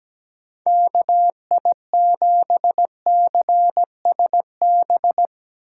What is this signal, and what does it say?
Morse code, heard as a keyed tone, about 17 wpm, 700 Hz. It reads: KI7CSB
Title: USA Call Signs - 6 Characters - 2x Character Spacing - 17wpm